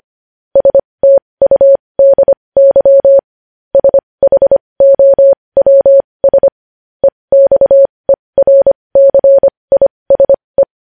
Morse code, heard as a keyed tone, about 25 wpm, spaced as for 20 wpm, 550 Hz. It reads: STUDY SHOWS EXERCISE
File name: podcast / News headlines-1x-25x20